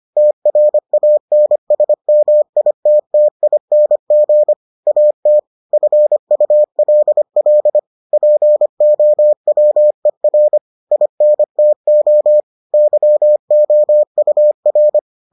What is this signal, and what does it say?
Morse code, heard as a keyed tone, about 25 wpm, 600 Hz. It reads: TRANSMITTING AT FULL POWER INTO YOUR